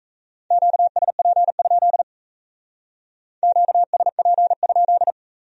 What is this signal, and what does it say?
Morse code, heard as a keyed tone, about 35 wpm, 700 Hz. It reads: QSP? QSP?